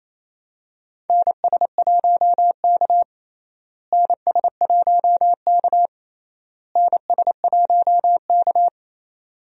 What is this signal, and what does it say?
Morse code, heard as a keyed tone, about 28 wpm, 700 Hz. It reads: NS1K NS1K NS1K
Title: USA Call Signs - 4 Characters - Repeated 3x - 28wpm